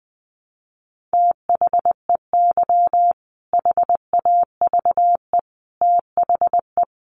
Morse code, heard as a keyed tone, about 20 wpm, 700 Hz. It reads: THEY HAVE THE
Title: Sentences from Top 1300 Words 20wpm